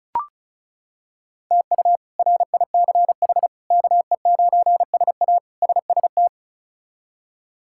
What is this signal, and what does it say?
Morse code, heard as a keyed tone, about 36 wpm, 700 Hz. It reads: TU RICH KE9SA SST
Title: K1USN SST QSO Practice - 35wpm